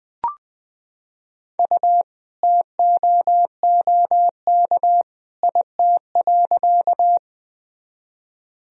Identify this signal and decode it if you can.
Morse code, heard as a keyed tone, about 20 wpm, 700 Hz. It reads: U TOOK IT.
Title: Sentences from Top 300 Words and Abbreviations 20wpm